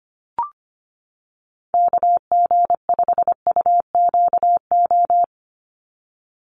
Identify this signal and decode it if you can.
Morse code, heard as a keyed tone, about 25 wpm, 700 Hz. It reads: KG5UQO
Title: USA Call Signs 25wpm